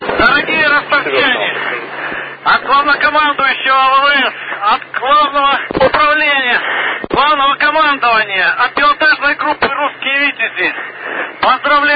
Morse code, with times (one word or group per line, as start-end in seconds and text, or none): none